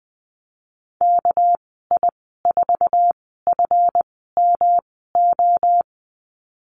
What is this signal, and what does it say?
Morse code, heard as a keyed tone, about 20 wpm, spaced as for 15 wpm, 700 Hz. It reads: KI4FMO
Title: 219 Lesson 35g - Number 0 - Call Signs 20wpm